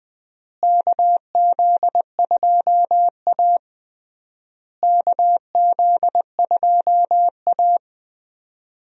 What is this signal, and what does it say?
Morse code, heard as a keyed tone, about 20 wpm, 700 Hz. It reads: KZ2A KZ2A
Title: USA Call Signs - 4 Characters - Repeated 3x - 20wpm